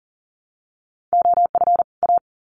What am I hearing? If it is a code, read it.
Morse code, heard as a keyed tone, about 40 wpm, 700 Hz. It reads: OF A
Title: Sets of 2 Words 40wpm